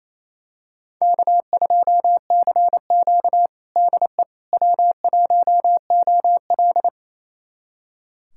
Morse code, heard as a keyed tone, about 27 wpm, 690 Hz. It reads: K2CQ DE W1OL
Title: USA Callsign DE Callsign - 4 Characters - 28wpm